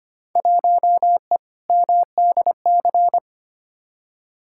Morse code, heard as a keyed tone, about 25 wpm, 700 Hz. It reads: 1E MDC
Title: Field Day Exchanges - 25wpm